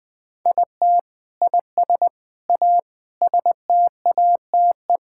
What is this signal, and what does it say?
Morse code, heard as a keyed tone, about 20 wpm, 700 Hz. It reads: IT IS A STATE
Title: Sentences from Top 1000 Words - Review Part 5 - 20wpm